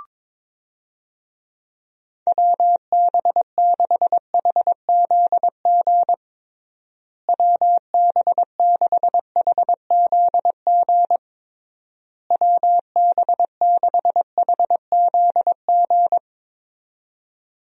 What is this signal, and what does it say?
Morse code, heard as a keyed tone, about 22 wpm, 700 Hz. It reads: WB6HZG WB6HZG WB6HZG
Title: USA Call Signs - 6 Characters - Repeated 3x - 22wpm